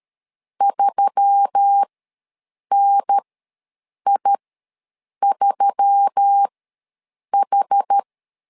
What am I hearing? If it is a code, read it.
Morse code, heard as a keyed tone, about 13 wpm, spaced as for 7 wpm, 800 Hz. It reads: 3NI3H
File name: LEZIONI / 4 Q1S3 ws 12wpm 2038